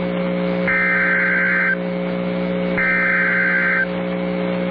Emergency Alert System (SAME) tones (header bursts), heard as harsh buzzing bursts. It is heard from 0.7 to 3.8 s.